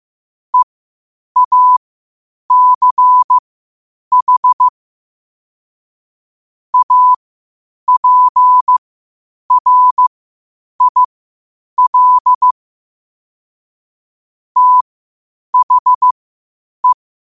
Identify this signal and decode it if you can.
Morse code, heard as a keyed tone, about 15 wpm, spaced as for 8 wpm, 1000 Hz. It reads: EACH APRIL THE
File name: qotdfiles / 8wpm